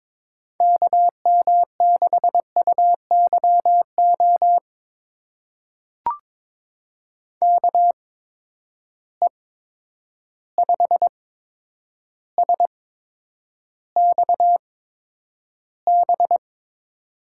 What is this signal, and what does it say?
Morse code, heard as a keyed tone, about 22 wpm, spaced as for 9 wpm, 700 Hz. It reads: KM6UYO KE5SXB
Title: USA Call Signs - 6 Characters - 8x Character Spacing - 22wpm